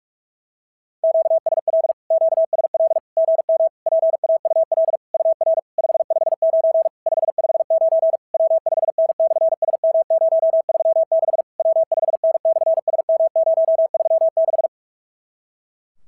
Morse code, heard as a keyed tone, about 44 wpm, 660 Hz. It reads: QSL QSL GM PAUL UR 559 559 W5N/SM036 W5N/SM036
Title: SOTA QSO Practice 45wpm